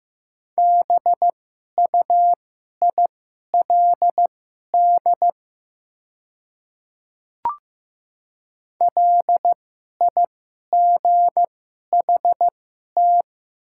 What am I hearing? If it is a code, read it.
Morse code, heard as a keyed tone, about 15 wpm, spaced as for 11 wpm, 700 Hz. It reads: BUILD LIGHT
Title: 2x Character Spacing - Top 200 Words - 15wpm